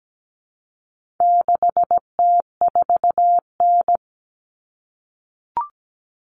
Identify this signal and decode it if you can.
Morse code, heard as a keyed tone, about 17 wpm, 700 Hz. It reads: 6T4N
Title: Common Cut Numbers - 4-Digit 17wpm